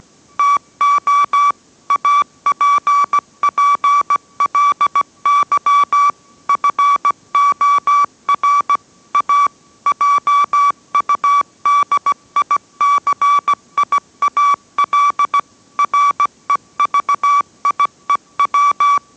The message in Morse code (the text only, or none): TO APPLY FOR A JUDICIAL REVIEW